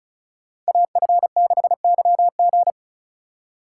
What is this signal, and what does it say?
Morse code, heard as a keyed tone, about 34 wpm, 700 Hz. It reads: AF6YG
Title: USA Call Signs - 5 Characters - 4x Character Spacing - 35wpm